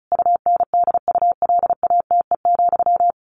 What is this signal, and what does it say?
Morse code, heard as a keyed tone, about 34 wpm, 700 Hz. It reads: UNDULATE,
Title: A Sound Of Thunder 35wpm